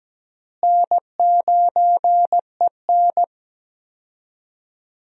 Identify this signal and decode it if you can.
Morse code, heard as a keyed tone, about 17 wpm, 700 Hz. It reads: N9EN
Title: USA Call Signs - 4 Characters - Repeated 3x - 17wpm